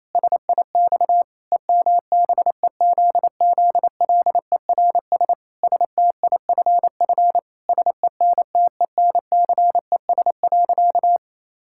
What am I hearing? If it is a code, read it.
Morse code, heard as a keyed tone, about 28 wpm, 700 Hz. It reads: SIX EMBEZZLERS STIFF SENTENCES.
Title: Pangram Challenge v2 - 28wpm